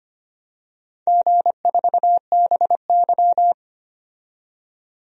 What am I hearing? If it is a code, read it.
Morse code, heard as a keyed tone, about 25 wpm, 700 Hz. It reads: G4BY
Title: DXCC-100 Call Signs - 4 Characters - 25wpm